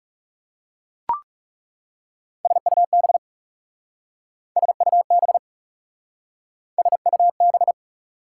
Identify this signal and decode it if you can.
Morse code, heard as a keyed tone, about 36 wpm, 690 Hz. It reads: SUB SUB SUB
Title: Speed-Racing Prefixes and Suffixes 30wpm